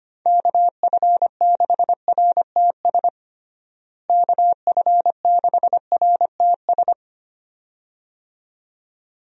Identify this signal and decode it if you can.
Morse code, heard as a keyed tone, about 25 wpm, 700 Hz. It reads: KF6RTS KF6RTS
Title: USA Call Signs - 6 Characters - Repeated 2x - 25wpm